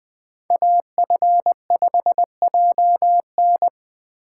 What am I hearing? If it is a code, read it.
Morse code, heard as a keyed tone, about 20 wpm, 700 Hz. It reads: AF5JN
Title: USA Call Signs - 5 Characters - 2x Character Spacing - 20wpm